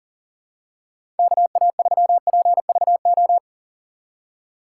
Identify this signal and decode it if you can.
Morse code, heard as a keyed tone, about 39 wpm, 690 Hz. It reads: KA3PVO
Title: USA Call Signs - 2x Character Spacing - 40wpm